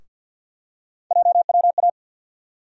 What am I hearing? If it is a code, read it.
Morse code, heard as a keyed tone, about 49 wpm, 700 Hz. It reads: JWA
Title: USA Call Sign Postfixes - 3 Letters 50wpm